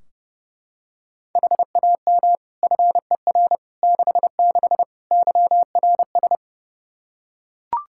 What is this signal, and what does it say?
Morse code, heard as a keyed tone, about 32 wpm, 700 Hz. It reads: HAM FER 66 YRS
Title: Practice QSO - Exchange 3 - More Info - 30wpm